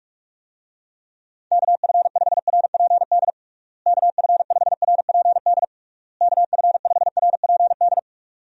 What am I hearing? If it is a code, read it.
Morse code, heard as a keyed tone, about 41 wpm, 700 Hz. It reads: KF5RPD KF5RPD KF5RPD
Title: USA Call Signs - 6 Characters - Repeated 3x - 45wpm